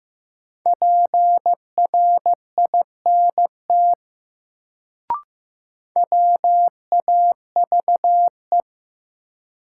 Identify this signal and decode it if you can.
Morse code, heard as a keyed tone, about 16 wpm, 700 Hz. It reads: PRINT WAVE
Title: Sentences from Top 1400 Words - New Word Drill - 15wpm